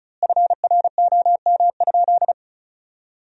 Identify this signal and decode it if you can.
Morse code, heard as a keyed tone, about 36 wpm, 700 Hz. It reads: FROM?
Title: 3x Word-Spacing - Sentences from Top 200 Words - 35wpm